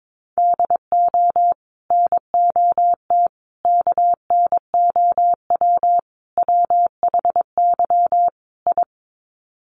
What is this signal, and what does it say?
Morse code, heard as a keyed tone, about 22 wpm, 700 Hz. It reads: DO NOT KNOW WHY I
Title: Sets of 5 Words Encore 22wpm